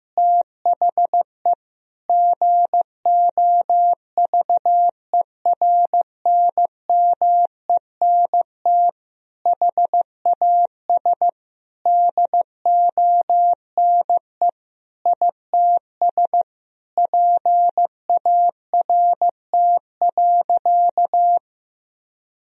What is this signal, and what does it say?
Morse code, heard as a keyed tone, about 15 wpm, 700 Hz. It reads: THE GOVERNMENT HAS DONE ITS PART.
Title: Sentences from Top 1000 Words - Review Part 3 - 15wpm